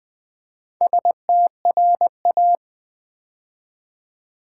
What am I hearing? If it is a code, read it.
Morse code, heard as a keyed tone, about 20 wpm, 700 Hz. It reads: STRA
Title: Build A Word - 2 Letters At A Time - 6-Letter Words - 20wpm